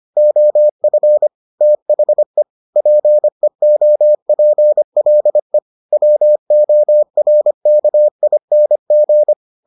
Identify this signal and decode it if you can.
Morse code, heard as a keyed tone, about 25 wpm, 600 Hz. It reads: OF THE PEOPLE WORKING